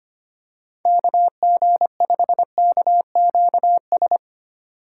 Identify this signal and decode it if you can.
Morse code, heard as a keyed tone, about 25 wpm, 700 Hz. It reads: KG5KQS
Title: USA Call Signs - 2x Character Spacing - 25wpm